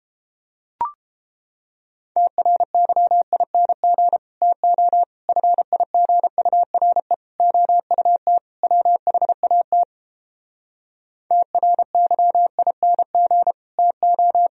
Morse code, heard as a keyed tone, about 29 wpm, 700 Hz. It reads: TRYING TO FIGURE OUT WHAT TRYING TO